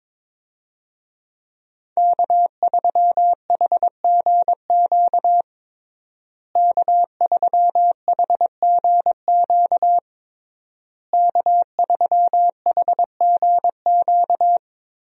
Morse code, heard as a keyed tone, about 22 wpm, 700 Hz. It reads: K3HGQ K3HGQ K3HGQ
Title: USA Call Signs - 5 Characters - Repeated 3x - 22wpm